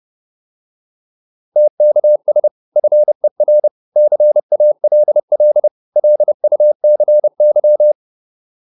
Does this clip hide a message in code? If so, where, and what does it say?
Yes: Morse code from 1.6 to 7.9 s, heard as a keyed tone, about 30 wpm, 600 Hz. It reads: TKS FER CALL LUCY